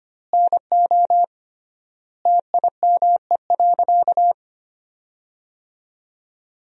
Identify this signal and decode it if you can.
Morse code, heard as a keyed tone, about 25 wpm, 700 Hz. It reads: NO TIME.
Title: 3x Word-Spacing - Sentences from Top 100 Words - 25wpm